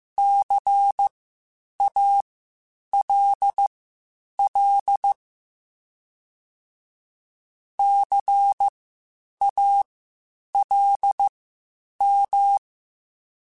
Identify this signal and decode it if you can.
Morse code, heard as a keyed tone, about 15 wpm, spaced as for 8 wpm, 800 Hz. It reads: CALL CALM
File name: Spaced Alphas / Spaced Alphas mp3s / Cs sp @13